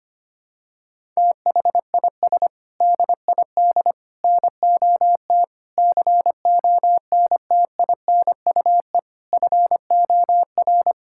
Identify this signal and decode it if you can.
Morse code, heard as a keyed tone, about 25 wpm, 700 Hz. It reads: THIS DID NOT CONTINUE FOR
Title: Sentences from Top 1000 Words - Review Part 6 - 25wpm